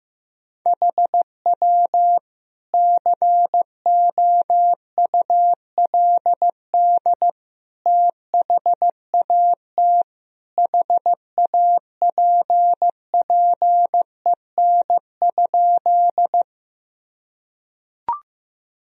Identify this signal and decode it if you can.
Morse code, heard as a keyed tone, about 16 wpm, 700 Hz. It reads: HW COULD THAT HAPPEN?
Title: Sentences from Top 400 Words and Abbreviations 15wpm